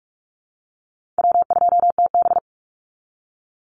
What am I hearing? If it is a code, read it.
Morse code, heard as a keyed tone, about 44 wpm, 700 Hz. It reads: W2TB
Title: Easy Call Signs Repeated 3x - 45wpm